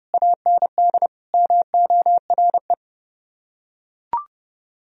Morse code, heard as a keyed tone, about 28 wpm, 700 Hz. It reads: AND MORE